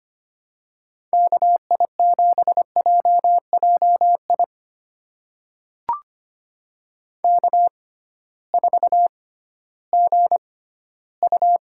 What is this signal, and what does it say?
Morse code, heard as a keyed tone, about 25 wpm, spaced as for 15 wpm, 700 Hz. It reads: KI7JJI K4GU
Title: USA Call Signs - 6x Character Spacing - 25wpm